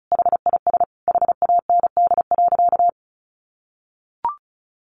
Morse code, heard as a keyed tone, about 34 wpm, 700 Hz. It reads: HIS HAND.
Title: Sentences from Top 1400 Words 35wpm